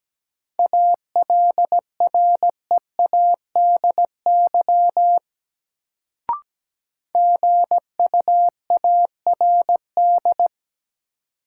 Morse code, heard as a keyed tone, about 17 wpm, 700 Hz. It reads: ALREADY GUARD
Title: Sentences from Top 700 Words - New Word Drill - 17wpm